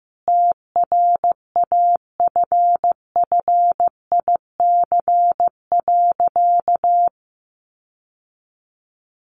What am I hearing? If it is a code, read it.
Morse code, heard as a keyed tone, about 15 wpm, 700 Hz. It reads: TRAFFIC.
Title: Sentences from Top 1300 Words 15wpm